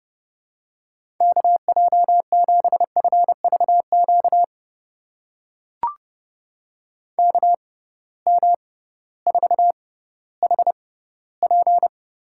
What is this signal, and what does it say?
Morse code, heard as a keyed tone, about 30 wpm, spaced as for 16 wpm, 700 Hz. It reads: KJ7FVQ KM4HP
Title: USA Call Signs - 6 Characters - 6x Character Spacing - 30wpm